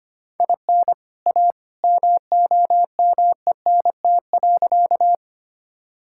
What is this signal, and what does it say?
Morse code, heard as a keyed tone, about 25 wpm, 700 Hz. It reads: IN A MOMENT.
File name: practice / 25wpm / Sentences with 2-Word Phrases - 25wpm